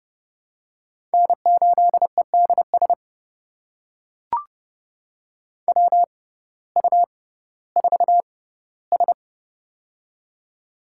Morse code, heard as a keyed tone, about 30 wpm, spaced as for 17 wpm, 700 Hz. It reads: N8EDS WU4S
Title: USA Call Signs - 6x Character Spacing - 30wpm